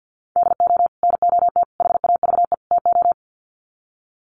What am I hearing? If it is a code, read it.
Morse code, heard as a keyed tone, about 49 wpm, 700 Hz. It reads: DO NOT HAVE TO
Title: Sets of 4 Words 50wpm